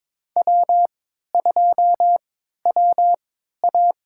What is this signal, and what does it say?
Morse code, heard as a keyed tone, about 22 wpm, spaced as for 13 wpm, 700 Hz. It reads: W2WA
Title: USA Call Signs - 4 Characters - 3x Character Spacing - 22wpm